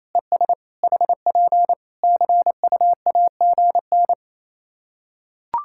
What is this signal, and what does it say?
Morse code, heard as a keyed tone, about 28 wpm, 700 Hz. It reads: ES HP CUAGN
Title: K1USN SST QSO Practice 28wpm